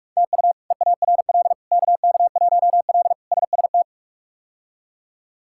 Morse code, heard as a keyed tone, about 42 wpm, 700 Hz. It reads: TU EARL KK1L SST
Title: K1USN SST QSO Practice - 45wpm